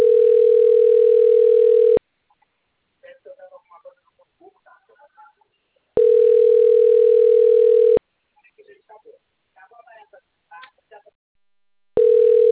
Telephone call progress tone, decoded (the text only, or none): ringback tone